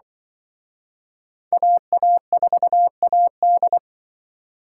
Morse code, heard as a keyed tone, about 25 wpm, 700 Hz. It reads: AA4AD
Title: Speed Racing - USA 2x2 Call Signs 20wpm